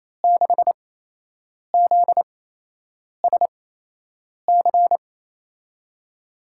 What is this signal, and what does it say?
Morse code, heard as a keyed tone, about 28 wpm, spaced as for 8 wpm, 700 Hz. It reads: 6ZSC
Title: USA Call Signs - 6 Characters - 8x Character Spacing - 28wpm